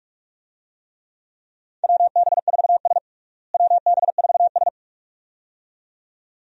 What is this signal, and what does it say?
Morse code, heard as a keyed tone, about 46 wpm, 700 Hz. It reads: WB4S WB4S
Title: USA Call Signs - 4 Characters - Repeated 2x - 45wpm